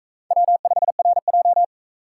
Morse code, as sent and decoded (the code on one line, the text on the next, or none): .-- ..... .-. .---
W5RJ